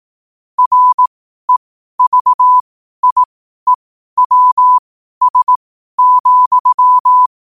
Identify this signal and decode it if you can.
Morse code, heard as a keyed tone, about 18 wpm, spaced as for 12 wpm, 1000 Hz. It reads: REVIEWS,
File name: qotdfiles / 12wpm